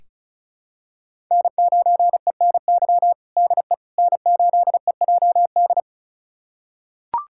Morse code, heard as a keyed tone, about 36 wpm, 700 Hz. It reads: N9ENY DE N8EJD
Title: USA Callsign DE Callsign - 5 Characters - 35wpm